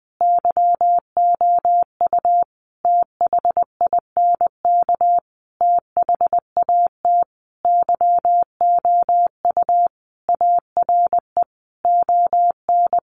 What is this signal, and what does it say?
Morse code, heard as a keyed tone, about 20 wpm, 700 Hz. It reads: YOU THINK THAT YOU ARE ON
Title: Sentences from Top 1400 Words 20wpm